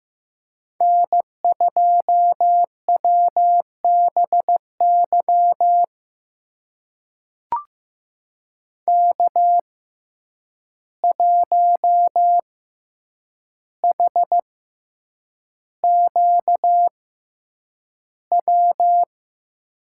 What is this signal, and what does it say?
Morse code, heard as a keyed tone, about 15 wpm, spaced as for 8 wpm, 700 Hz. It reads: N2WBY K1HQW
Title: USA Call Signs - 5 Characters - 6x Character Spacing - 15wpm